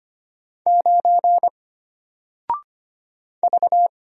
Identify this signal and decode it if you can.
Morse code, heard as a keyed tone, about 25 wpm, 700 Hz. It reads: 9 V
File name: Single-Letter-Number-25wpm